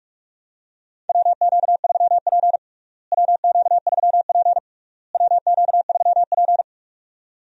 Morse code, heard as a keyed tone, about 46 wpm, 700 Hz. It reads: WQ3P WQ3P WQ3P
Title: USA Call Signs - 4 Characters - Repeated 3x - 45wpm